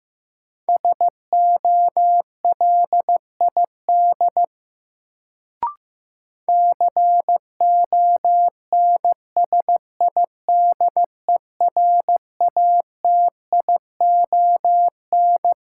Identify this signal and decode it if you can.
Morse code, heard as a keyed tone, about 16 wpm, 700 Hz. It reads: SOLID CONSIDERATION
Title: Sentences from Top 1500 Words - New Word Drill Alt - 15wpm